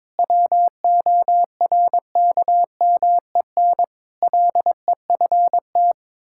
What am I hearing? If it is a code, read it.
Morse code, heard as a keyed tone, about 22 wpm, 700 Hz. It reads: WORKMEN LEFT
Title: Pangram Challenge v2 - 22wpm